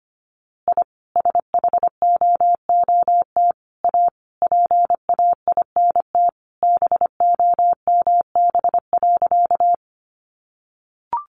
Morse code, heard as a keyed tone, about 25 wpm, 700 Hz. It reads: I SHOOT A PAINT BOMB.
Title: A Sound Of Thunder 25wpm